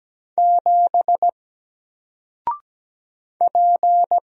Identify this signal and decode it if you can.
Morse code, heard as a keyed tone, about 17 wpm, 700 Hz. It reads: 7 P
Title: Single Letter-Number Alt 17wpm